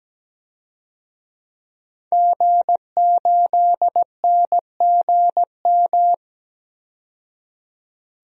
Morse code, heard as a keyed tone, about 17 wpm, 700 Hz. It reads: G8NGM
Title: DXCC-100 Call Signs - 5 Characters - 17wpm